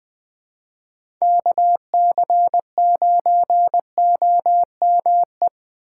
Morse code, heard as a keyed tone, about 20 wpm, 700 Hz. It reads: KC9OME
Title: USA Call Signs - 4x Character Spacing - 20wpm